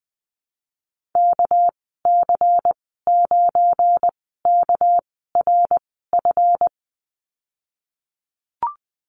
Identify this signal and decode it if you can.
Morse code, heard as a keyed tone, about 20 wpm, spaced as for 15 wpm, 700 Hz. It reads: KC9KRF
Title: 238 Lesson 38g - Letter q - Call Signs 20wpm